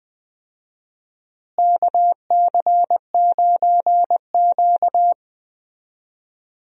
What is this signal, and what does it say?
Morse code, heard as a keyed tone, about 20 wpm, 700 Hz. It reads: KC9Q
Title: USA Call Signs - 4 Characters - Repeated 3x - 20wpm